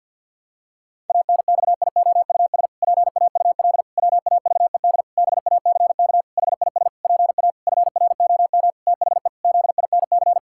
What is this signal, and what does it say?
Morse code, heard as a keyed tone, about 50 wpm, 700 Hz. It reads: ANXIOUS PAUL WAVED BACK HIS PA FROM THE ZINC
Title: Pangram Challenge v2 - 50wpm